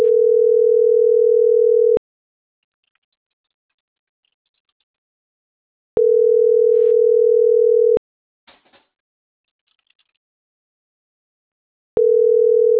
Ringback tone (North America), 440 + 480 Hz.
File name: CENTRAL MAYAB / out fernanda.franco